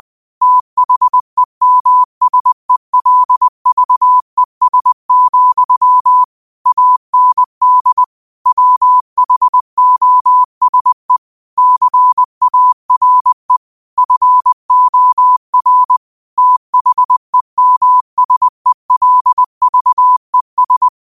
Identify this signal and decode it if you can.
Morse code, heard as a keyed tone, about 20 wpm, 1000 Hz. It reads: THEMSELVES, AND WHOSE CARE FOR THEMSELVES
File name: qotdfiles / 20wpm